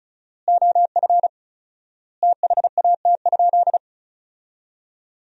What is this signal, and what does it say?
Morse code, heard as a keyed tone, about 36 wpm, 700 Hz. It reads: OF THAT?
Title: 4x Word-Spacing - Sentences from Top 300 Words - 35wpm